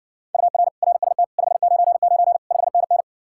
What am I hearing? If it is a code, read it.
Morse code, heard as a keyed tone, about 67 wpm, 700 Hz. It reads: UR RST 599 5NN